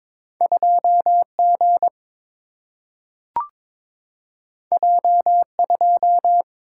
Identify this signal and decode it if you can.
Morse code, heard as a keyed tone, about 22 wpm, 700 Hz. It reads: 2G J2
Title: Single-Letter Single-Number 22wpm